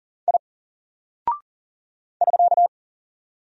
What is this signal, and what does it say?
Morse code, heard as a keyed tone, about 40 wpm, 700 Hz. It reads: I <SK>
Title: Single Character (Letter:Number:Punctuation:Prosign) 40wpm